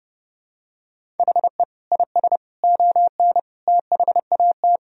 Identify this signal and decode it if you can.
Morse code, heard as a keyed tone, about 30 wpm, 700 Hz. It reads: HE IS ON THAT
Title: Sentences from Top 1000 Words - Review Part 1 - 30wpm